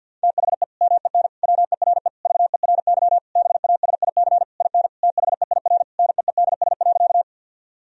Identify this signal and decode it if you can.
Morse code, heard as a keyed tone, about 52 wpm, 700 Hz. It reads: THE MEN WERE VERY BASIC IN THEIR NEEDS.
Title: Sentences from Top 1000 Words - Review Part 3 - 50wpm